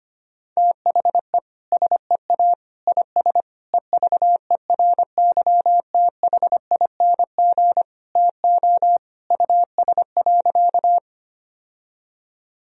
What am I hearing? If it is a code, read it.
Morse code, heard as a keyed tone, about 25 wpm, 700 Hz. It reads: THE SEA IS EVERYTHING TO US.
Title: Sentences from Top 1000 Words - Review Part 1 - 25wpm